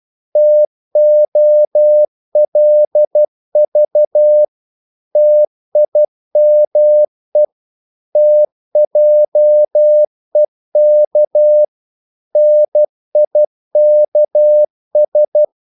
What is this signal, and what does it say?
Morse code, heard as a keyed tone, about 12 wpm, 600 Hz. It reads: TOLV TIME TJEK NIKS
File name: Korte ord DK 12wpm